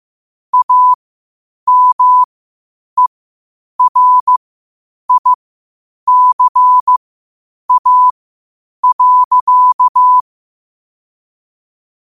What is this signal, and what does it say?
Morse code, heard as a keyed tone, about 15 wpm, spaced as for 8 wpm, 1000 Hz. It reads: AMERICA.
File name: qotdfiles / 8wpm